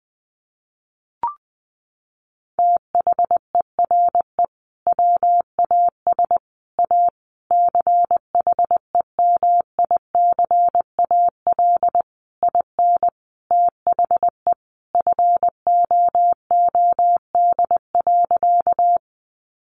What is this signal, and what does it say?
Morse code, heard as a keyed tone, about 19 wpm, 700 Hz. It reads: THERE WAS A CHEMICAL IN THE FOOD.
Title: Sentences from Top 1200 Words 20wpm